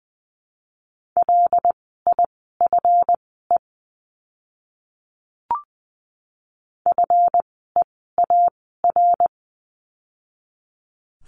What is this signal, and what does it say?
Morse code, heard as a keyed tone, about 20 wpm, spaced as for 14 wpm, 700 Hz. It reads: LIFE FEAR
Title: 139 Lesson 23f - Letter f - Words 20wpm